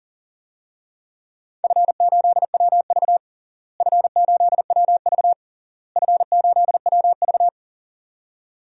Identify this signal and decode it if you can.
Morse code, heard as a keyed tone, about 40 wpm, 700 Hz. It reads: F8WV F8WV F8WV
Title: DXCC-100 Call Signs - 4 Characters - Repeated 3x - 40wpm